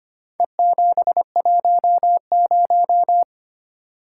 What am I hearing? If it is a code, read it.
Morse code, heard as a keyed tone, about 25 wpm, 700 Hz. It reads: E710
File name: practice / 25wpm / Four Letter-Number Combinations - 25wpm